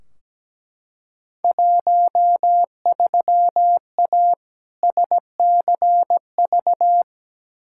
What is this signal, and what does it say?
Morse code, heard as a keyed tone, about 17 wpm, 700 Hz. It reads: 13A SCV
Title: Field Day Exchanges Encore - 17wpm